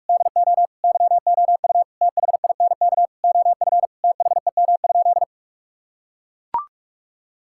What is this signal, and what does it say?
Morse code, heard as a keyed tone, about 41 wpm, 700 Hz. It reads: DO YOU THINK OF THEM?
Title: Sentences from Top 1600 Words 45wpm